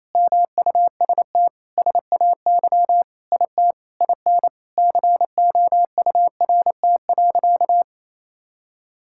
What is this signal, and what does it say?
Morse code, heard as a keyed tone, about 28 wpm, 700 Hz. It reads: MUST SAY IT IN COURT.
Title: Sentences from Top 1600 Words 28wpm